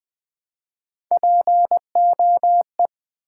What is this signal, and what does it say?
Morse code, heard as a keyed tone, about 20 wpm, 700 Hz. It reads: POE
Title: Build A Word - 1 Letter At A Time - 4-Letter Words - 20wpm